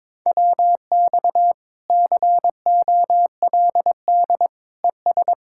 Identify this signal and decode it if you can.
Morse code, heard as a keyed tone, about 22 wpm, 700 Hz. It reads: WX COLD ES
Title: Speed-Racing Common QSO Phrases - 22wpm